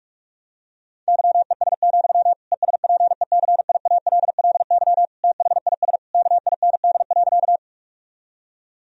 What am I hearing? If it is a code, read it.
Morse code, heard as a keyed tone, about 41 wpm, 700 Hz. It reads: YES, ESPECIALLY THIS KIND.
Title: Sentences from Top 1000 Words - Review Part 4 - 45wpm